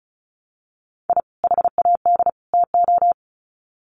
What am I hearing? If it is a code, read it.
Morse code, heard as a keyed tone, about 34 wpm, 700 Hz. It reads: I HAD TO
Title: Sets of 3 Words 35wpm